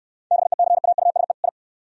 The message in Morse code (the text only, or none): BECAUSE I